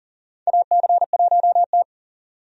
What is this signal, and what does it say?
Morse code, heard as a keyed tone, about 39 wpm, 700 Hz. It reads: AC1T
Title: USA Call Signs - 4 Characters - 40wpm